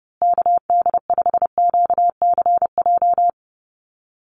Morse code, heard as a keyed tone, about 30 wpm, 700 Hz. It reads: KD5QCJ
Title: USA Call Signs Repeated 3x 30wpm